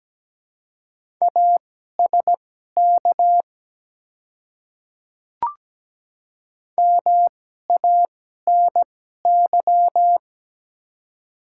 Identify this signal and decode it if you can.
Morse code, heard as a keyed tone, about 17 wpm, spaced as for 12 wpm, 700 Hz. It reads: ASK MANY
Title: 2x Character Spacing - Top 200 Words - 17wpm